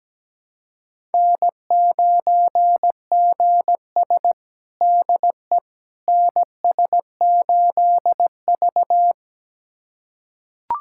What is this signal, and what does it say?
Morse code, heard as a keyed tone, about 17 wpm, 700 Hz. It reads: N9GS DE NS8V
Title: USA Callsign DE Callsign - 4 Characters - 17wpm